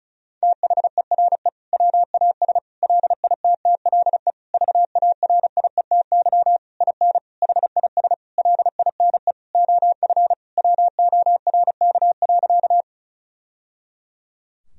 Morse code, heard as a keyed tone, about 34 wpm, 700 Hz. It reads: THERE WAS LITTLE VARIETY IN HIS LINE OF WORK.
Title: Sentences from Top 1000 Words - Review Part 4 - 35wpm